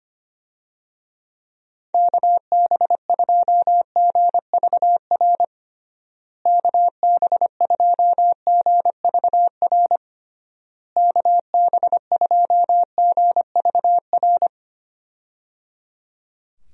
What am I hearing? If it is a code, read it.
Morse code, heard as a keyed tone, about 25 wpm, 700 Hz. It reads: KB2GVR KB2GVR KB2GVR
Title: USA Call Signs - 6 Characters - Repeated 3x - 25wpm